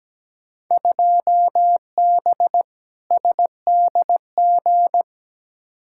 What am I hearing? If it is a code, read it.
Morse code, heard as a keyed tone, about 17 wpm, 700 Hz. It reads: 2B SDG
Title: Field Day Exchanges Encore - 17wpm